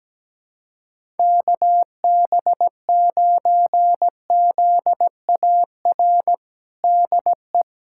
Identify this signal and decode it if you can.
Morse code, heard as a keyed tone, about 17 wpm, 700 Hz. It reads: KB9ZAR DE
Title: USA Callsign DE Callsign - 6 Characters - 17wpm